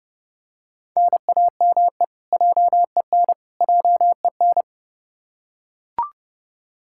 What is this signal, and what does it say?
Morse code, heard as a keyed tone, about 31 wpm, 700 Hz. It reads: NAME JEN JEN
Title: K1USN SST QSO Practice 30wpm